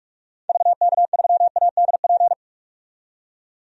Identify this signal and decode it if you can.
Morse code, heard as a keyed tone, about 46 wpm, 700 Hz. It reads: VK3ADP
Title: DXCC-100 Call Signs - 6 Characters - 45wpm